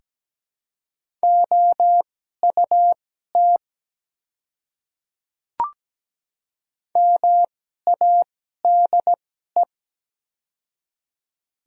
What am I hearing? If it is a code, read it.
Morse code, heard as a keyed tone, about 17 wpm, spaced as for 12 wpm, 700 Hz. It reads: OUT MADE
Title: 2x Character Spacing - Top 200 Words - 17wpm